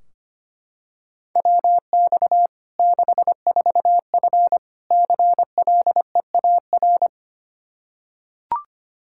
Morse code, heard as a keyed tone, about 25 wpm, 700 Hz. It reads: WX 64F CLEAR
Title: Practice QSO - Exchange 2 - Info - 25wpm